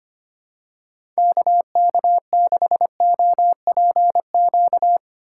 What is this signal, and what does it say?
Morse code, heard as a keyed tone, about 25 wpm, 700 Hz. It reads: KK6OPQ
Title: USA Call Signs - 6 Characters - 8x Character Spacing - 25wpm